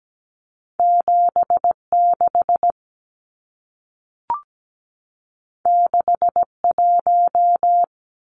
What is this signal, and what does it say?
Morse code, heard as a keyed tone, about 17 wpm, 700 Hz. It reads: 76 61
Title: Numbers - 2-Digit 17wpm